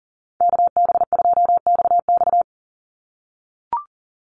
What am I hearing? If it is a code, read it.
Morse code, heard as a keyed tone, about 40 wpm, 700 Hz. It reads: KB2XX
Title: NA SOTA Chaser Callsigns 40wpm